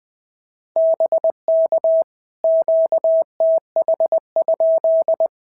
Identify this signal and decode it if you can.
Morse code, heard as a keyed tone, about 18 wpm, 650 Hz. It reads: BK QTH?
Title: POTA QSO Practice - 20wpm